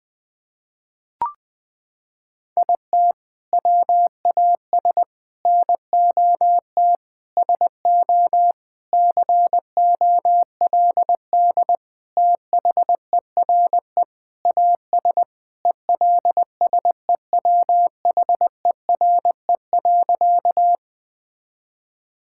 Morse code, heard as a keyed tone, about 20 wpm, 700 Hz. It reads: IT WAS NOT SO COLD THERE AS ELSEWHERE.
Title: Sentences from Top 1600 Words 20wpm